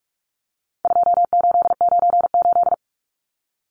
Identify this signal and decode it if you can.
Morse code, heard as a keyed tone, about 44 wpm, 700 Hz. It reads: 2898
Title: Numbers - 4-Digit 45wpm